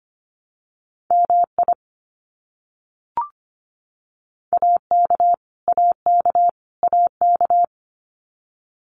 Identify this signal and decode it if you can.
Morse code, heard as a keyed tone, about 25 wpm, 700 Hz. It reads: MI AK AK AK
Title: US States Repeated 3x 25wpm